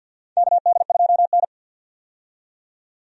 Morse code, heard as a keyed tone, about 53 wpm, 700 Hz. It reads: KD2N
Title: USA Call Signs - 4 Characters - 50wpm